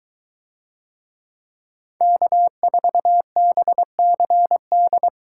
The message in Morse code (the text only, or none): K4BCD